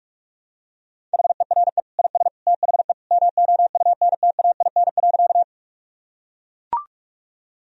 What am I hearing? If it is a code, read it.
Morse code, heard as a keyed tone, about 43 wpm, 700 Hz. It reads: HERE IS THE MOUNTAIN.
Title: Sentences from Top 1000 Words - Review Part 1 - 45wpm